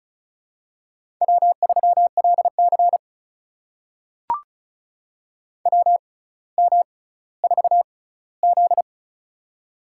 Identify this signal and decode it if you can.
Morse code, heard as a keyed tone, about 36 wpm, spaced as for 18 wpm, 700 Hz. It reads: W3LC WM4Z
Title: USA Call Signs - 4 Characters - 6x Character Spacing - 35wpm